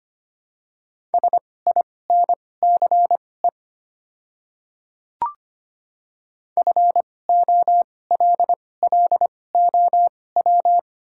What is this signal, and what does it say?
Morse code, heard as a keyed tone, about 25 wpm, spaced as for 18 wpm, 700 Hz. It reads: SINCE FOLLOW
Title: 2x Character Spacing - Top 500 Words - 25wpm